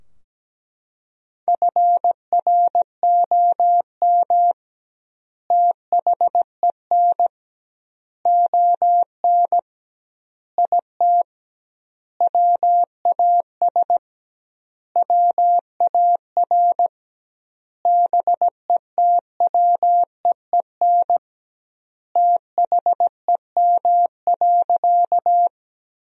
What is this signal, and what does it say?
Morse code, heard as a keyed tone, about 17 wpm, 700 Hz. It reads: FROM THEN ON IT WAS WAR BETWEEN THEM.
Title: 2x Word-Spacing - Sentences from Top 400 Words - 17wpm